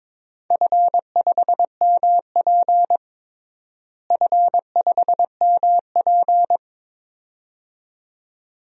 Morse code, heard as a keyed tone, about 22 wpm, 700 Hz. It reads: F5MP F5MP
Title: DXCC-100 Call Signs - 4 Characters - Repeated 2x - 22wpm